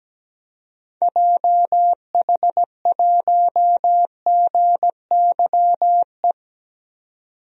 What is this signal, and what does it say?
Morse code, heard as a keyed tone, about 17 wpm, 700 Hz. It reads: JH1GYE
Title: DXCC-100 Call Signs - 6 Characters - 17wpm